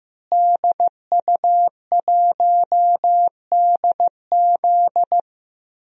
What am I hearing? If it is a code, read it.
Morse code, heard as a keyed tone, about 14 wpm, 700 Hz. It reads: DU1DZ
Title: DXCC-100 Call Signs - 5 Characters - Repeated 3x - 15wpm